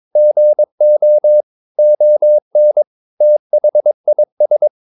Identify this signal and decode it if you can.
Morse code, heard as a keyed tone, about 22 wpm, 600 Hz. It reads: GO ON THIS
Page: Romanen Never af Ken Follet på engelsk.